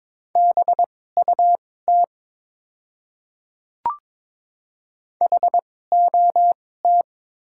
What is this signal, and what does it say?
Morse code, heard as a keyed tone, about 22 wpm, spaced as for 16 wpm, 700 Hz. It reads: BUT HOT
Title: 2x Character Spacing - Top 100 Words - 22wpm